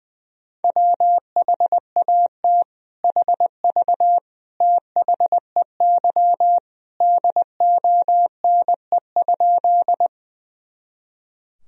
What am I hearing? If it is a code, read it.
Morse code, heard as a keyed tone, about 21 wpm, 700 Hz. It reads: WHAT HV THEY DONE?
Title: Sentences from Top 500 Words and Abbreviations 20wpm